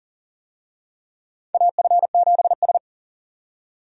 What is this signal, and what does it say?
Morse code, heard as a keyed tone, about 39 wpm, 700 Hz. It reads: AF7S
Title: USA Call Signs - 4 Characters - 40wpm